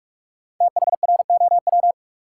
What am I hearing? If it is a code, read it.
Morse code, heard as a keyed tone, about 47 wpm, 700 Hz. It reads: THROW
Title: Five Letter Words - 45wpm